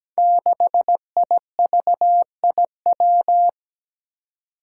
Morse code, heard as a keyed tone, about 17 wpm, 700 Hz. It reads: 6IVIW
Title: Five Challenging Letter-Number Combinations 17wpm